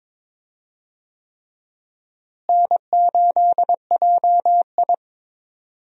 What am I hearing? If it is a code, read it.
Morse code, heard as a keyed tone, about 22 wpm, 700 Hz. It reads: N8JI
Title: USA Call Signs - 4 Characters - Repeated 3x - 22wpm